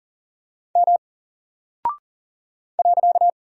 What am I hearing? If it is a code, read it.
Morse code, heard as a keyed tone, about 38 wpm, 700 Hz. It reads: M .
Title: Single Character (Letter:Number:Punctuation:Prosign) Alt 40wpm